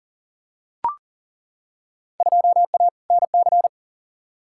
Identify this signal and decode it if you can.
Morse code, heard as a keyed tone, about 40 wpm, 700 Hz. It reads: 2A NC
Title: Field Day Exchanges Encore - 40wpm